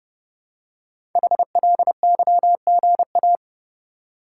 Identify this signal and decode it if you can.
Morse code, heard as a keyed tone, about 30 wpm, 700 Hz. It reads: HLYGA
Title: Code Groups - 5 Letters - 30wpm